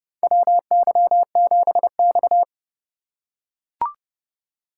Morse code, heard as a keyed tone, about 30 wpm, 700 Hz. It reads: WY7X
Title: USA Call Signs - 4 Characters - 2x Character Spacing - 30wpm